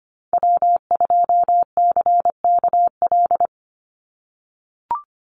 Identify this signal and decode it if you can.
Morse code, heard as a keyed tone, about 25 wpm, 700 Hz. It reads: W2CKL
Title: NA SOTA Activator Callsigns 25wpm